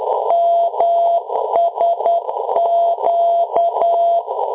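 Morse code, heard as a keyed tone, about 10 wpm, 680 Hz. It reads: MSQ
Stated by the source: Low frequency aviation radio beacon; aviation LF/MF beacon.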